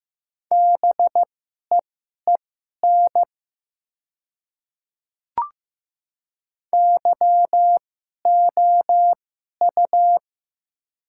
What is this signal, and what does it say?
Morse code, heard as a keyed tone, about 15 wpm, spaced as for 11 wpm, 700 Hz. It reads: BEEN YOU
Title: 2x Character Spacing - Top 200 Words - 15wpm